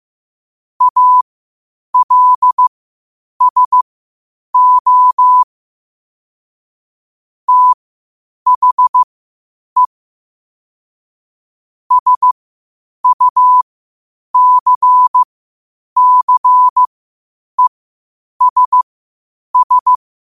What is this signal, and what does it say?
Morse code, heard as a keyed tone, about 15 wpm, spaced as for 8 wpm, 1000 Hz. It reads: ALSO THE SUCCESS
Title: 8 WPM morse code quotes for Thu, 29 Feb 2024